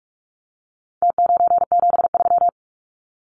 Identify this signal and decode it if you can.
Morse code, heard as a keyed tone, about 44 wpm, 700 Hz. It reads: T973
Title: Numbers - Contest Style - 4-Digit 45wpm